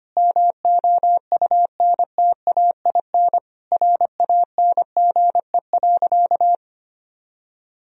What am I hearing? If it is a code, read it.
Morse code, heard as a keyed tone, about 25 wpm, 700 Hz. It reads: MOUNTAIN RANGE.
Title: Sentences from Top 1000 Words - Review Part 4 - 25wpm